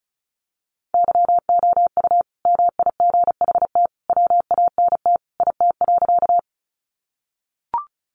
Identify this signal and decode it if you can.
Morse code, heard as a keyed tone, about 34 wpm, 700 Hz. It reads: YOU MIGHT WANT IT.
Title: Sentences from Top 300 Words 35wpm